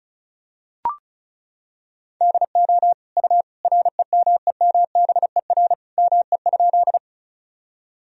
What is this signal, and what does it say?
Morse code, heard as a keyed tone, about 36 wpm, 700 Hz. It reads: DO U REMEMBER ME?
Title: Sentences from Top 400 Words and Abbreviations 35wpm